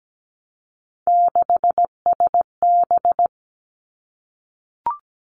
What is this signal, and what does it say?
Morse code, heard as a keyed tone, about 17 wpm, 700 Hz. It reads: 6SB
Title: Three Challenging Letter-Number Combinations 17wpm